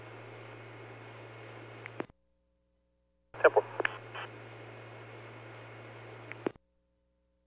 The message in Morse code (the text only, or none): none